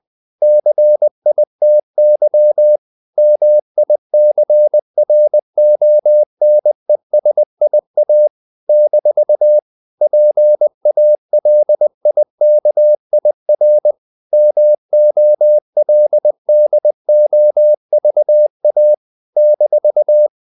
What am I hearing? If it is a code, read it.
Morse code, heard as a keyed tone, about 20 wpm, 600 Hz. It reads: CITY MICRONESIA - PALIKIR MOLDOVA -
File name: CW / Practice / NationalCapitals20